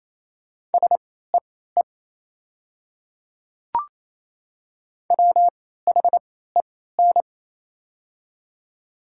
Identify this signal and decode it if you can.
Morse code, heard as a keyed tone, about 28 wpm, spaced as for 16 wpm, 700 Hz. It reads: SEE WHEN
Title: 3x Character Spacing - Top 200 Words - 28wpm